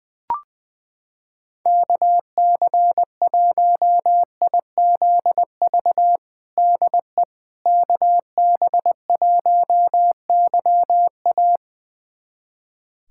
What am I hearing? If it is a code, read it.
Morse code, heard as a keyed tone, about 20 wpm, 700 Hz. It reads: KC1IZV DE KB1YA
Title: USA Callsign DE Callsign 20wpm